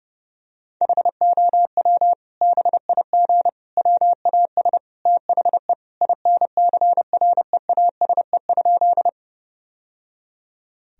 Morse code, heard as a keyed tone, about 30 wpm, 700 Hz. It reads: HOW BIG WAS THE INCREASE?
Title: Sentences from Top 1000 Words - Review Part 4 - 30wpm